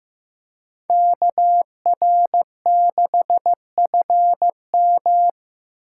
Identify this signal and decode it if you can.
Morse code, heard as a keyed tone, about 15 wpm, 700 Hz. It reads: KR6FM
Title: USA Call Signs - 5 Characters - 8x Character Spacing - 15wpm